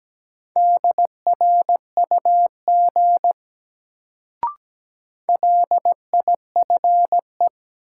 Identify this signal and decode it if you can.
Morse code, heard as a keyed tone, about 17 wpm, 700 Hz. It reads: DRUG LIFE
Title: Sets of 5 Words - New Word Drill - 17wpm